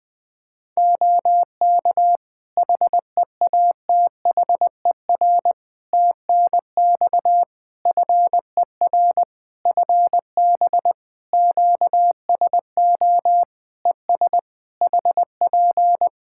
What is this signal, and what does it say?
Morse code, heard as a keyed tone, about 20 wpm, 700 Hz. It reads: OK HEATHER TNX FER FB QSO ES HP
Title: K1USN SST QSO Practice 20wpm